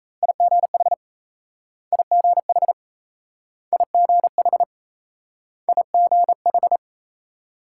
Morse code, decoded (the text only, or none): IGH IGH IGH IGH